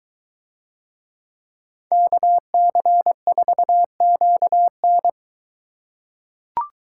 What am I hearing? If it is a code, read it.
Morse code, heard as a keyed tone, about 24 wpm, 700 Hz. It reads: KC4QN